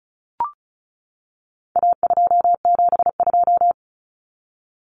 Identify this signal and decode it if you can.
Morse code, heard as a keyed tone, about 34 wpm, 700 Hz. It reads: A272
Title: Common Cut Numbers - 4-Digit 35wpm